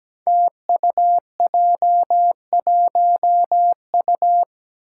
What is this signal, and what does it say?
Morse code, heard as a keyed tone, about 17 wpm, 700 Hz. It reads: TUJ1U
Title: Five Challenging Letter-Number Combinations 17wpm